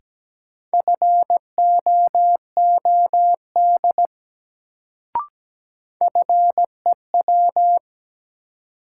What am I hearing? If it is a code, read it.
Morse code, heard as a keyed tone, about 17 wpm, 700 Hz. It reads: FOOD FEW
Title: Sentences from Top 300 Words - New Word Drill - 17wpm